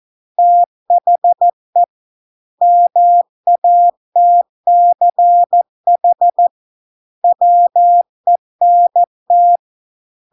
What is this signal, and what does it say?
Morse code, heard as a keyed tone, about 14 wpm, 700 Hz. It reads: THE MATCH WENT